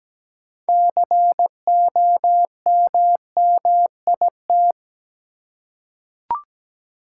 Morse code, heard as a keyed tone, about 17 wpm, 700 Hz. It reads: COMMIT
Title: Six Letter Words - 17wpm